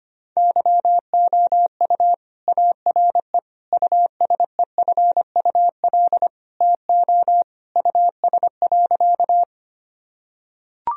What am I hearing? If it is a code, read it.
Morse code, heard as a keyed tone, about 25 wpm, 700 Hz. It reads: YOU ARE USEFUL TO US.
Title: Sentences from Top 1000 Words - Review Part 4 - 25wpm